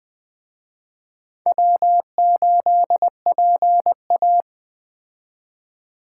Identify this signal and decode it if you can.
Morse code, heard as a keyed tone, about 20 wpm, 700 Hz. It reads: W8PA
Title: Speed Racing - USA 1x2 Call Signs 17wpm